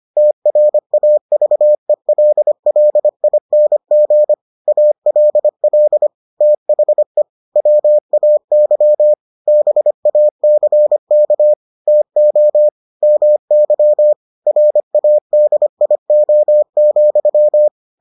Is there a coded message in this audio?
Yes: Morse code from 0.2 to 17.7 s, heard as a keyed tone, about 25 wpm, 600 Hz. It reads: TRAVELLING ALL THE WAY BACK TO MY RADIO,